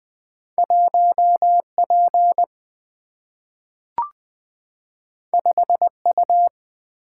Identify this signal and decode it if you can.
Morse code, heard as a keyed tone, about 20 wpm, 700 Hz. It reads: 1 P 5 U
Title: Single-Letter Single-Number 20wpm